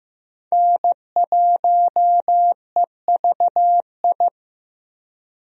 Morse code, heard as a keyed tone, about 15 wpm, 700 Hz. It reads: N1EVI
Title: USA Call Signs - 5 Characters - 15wpm